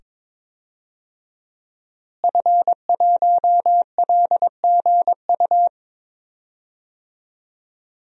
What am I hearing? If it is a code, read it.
Morse code, heard as a keyed tone, about 22 wpm, 700 Hz. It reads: F1LGU
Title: DXCC-100 Call Signs - 5 Characters - Repeated 2x - 22wpm